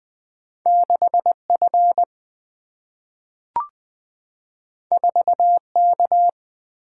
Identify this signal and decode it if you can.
Morse code, heard as a keyed tone, about 20 wpm, 700 Hz. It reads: 6F 4K
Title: Single-Letter Single-Number 20wpm